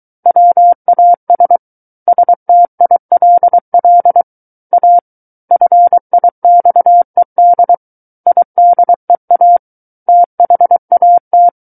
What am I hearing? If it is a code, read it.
Morse code, heard as a keyed tone, about 23 wpm, 700 Hz. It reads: WAS STILL A FIXED IDEA THAT
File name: Book One / 14-Chapter 14 - 23 WPM